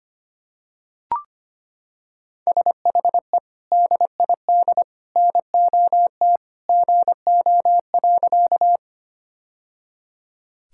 Morse code, heard as a keyed tone, about 25 wpm, 700 Hz. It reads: SHE DID NOT GO.
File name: practice / 25wpm / Sentences with 3-Word Phrases - 25wpm